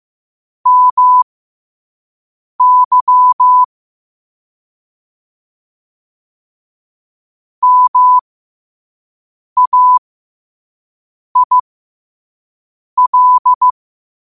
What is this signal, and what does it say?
Morse code, heard as a keyed tone, about 15 wpm, spaced as for 5 wpm, 1000 Hz. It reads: MY MAIL
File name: qotdfiles / 5wpm